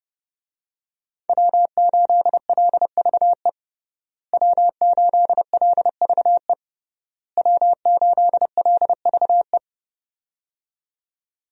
Morse code, heard as a keyed tone, about 30 wpm, 700 Hz. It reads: W8LVE W8LVE W8LVE
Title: USA Call Signs - 5 Characters - Repeated 3x - 30wpm